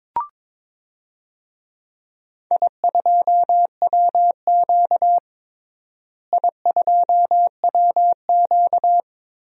Morse code, heard as a keyed tone, about 22 wpm, 700 Hz. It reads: I2WQ I2WQ
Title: DXCC-100 Call Signs - 4 Characters - Repeated 3x - 22wpm